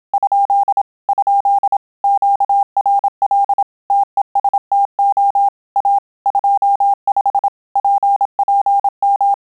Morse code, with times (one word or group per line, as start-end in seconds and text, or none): 0.1-0.8 ?
1.1-1.8 ?
2.0-3.6 QRL
3.9-5.5 TESTO
5.8-6.0 A
6.3-7.5 25
7.8-9.3 PPM